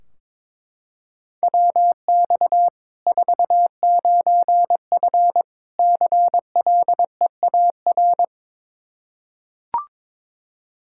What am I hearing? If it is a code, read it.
Morse code, heard as a keyed tone, about 22 wpm, 700 Hz. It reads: WX 49F CLEAR
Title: Practice QSO - Exchange 2 - Info - 22wpm